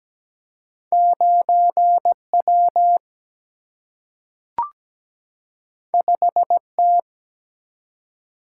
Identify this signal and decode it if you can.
Morse code, heard as a keyed tone, about 17 wpm, 700 Hz. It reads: 9 W 5 T
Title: Single-Letter Single-Number 17wpm